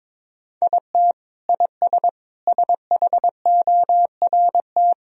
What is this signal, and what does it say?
Morse code, heard as a keyed tone, about 22 wpm, 700 Hz. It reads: IT IS SHORT
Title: Sentences from Top 1000 Words - Review Part 2 - 22wpm